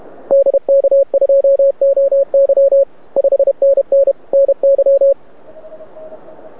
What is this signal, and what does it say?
Morse code, heard as a keyed tone, about 29 wpm, 550 Hz. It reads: DK2OY 5NN NY
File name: rec contest / arrl cw 2018 / test 32